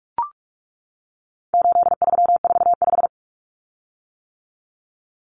Morse code, heard as a keyed tone, about 44 wpm, 700 Hz. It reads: 8345
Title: Common Cut Numbers - 4-Digit 45wpm